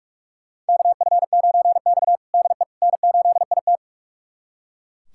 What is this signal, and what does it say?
Morse code, heard as a keyed tone, about 41 wpm, 700 Hz. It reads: KF9X DE N8IT
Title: USA Callsign DE Callsign - 4 Characters - 45wpm